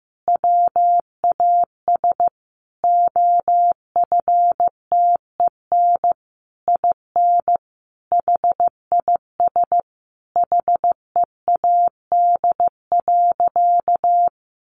Morse code, heard as a keyed tone, about 15 wpm, 700 Hz. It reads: WAS OFTEN IN HIS HEAD.
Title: Sentences from Top 1400 Words 15wpm